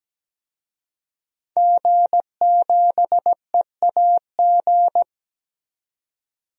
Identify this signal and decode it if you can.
Morse code, heard as a keyed tone, about 17 wpm, 700 Hz. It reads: G7EAG
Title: DXCC-100 Call Signs - 5 Characters - 17wpm